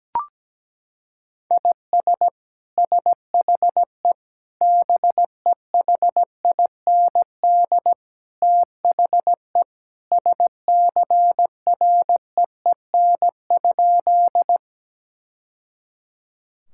Morse code, heard as a keyed tone, about 17 wpm, 700 Hz. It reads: IS SHE BEHIND THE SCREEN?
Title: Sentences from Top 1000 Words - Review Part 6 - 17wpm